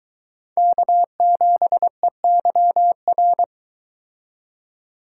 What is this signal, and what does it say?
Morse code, heard as a keyed tone, about 23 wpm, 700 Hz. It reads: K7EYR